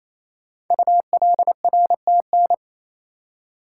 Morse code, heard as a keyed tone, about 27 wpm, 700 Hz. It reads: ULRTN
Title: Code Groups - 5 Letters - 28wpm